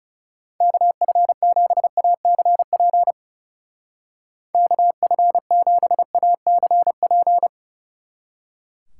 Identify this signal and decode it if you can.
Morse code, heard as a keyed tone, about 37 wpm, 700 Hz. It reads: KF7ACP KF7ACP
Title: Speed Racing - USA 2x3 Call Signs 30wpm